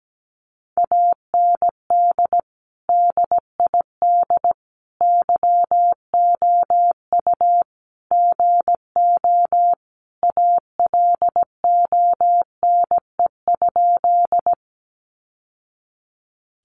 Morse code, heard as a keyed tone, about 17 wpm, 700 Hz. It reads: AND DID YOU GO ALONE?
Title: Sentences from Top 1000 Words 17wpm